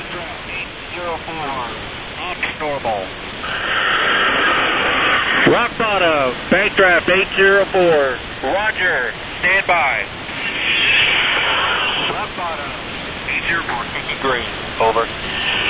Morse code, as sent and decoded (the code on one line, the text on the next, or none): none